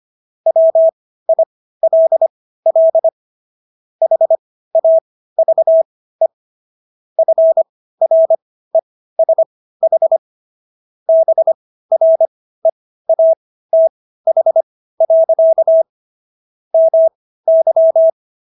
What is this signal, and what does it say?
Morse code, heard as a keyed tone, about 25 wpm, spaced as for 15 wpm, 650 Hz. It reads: WILL HAVE FRESH BREATH. MY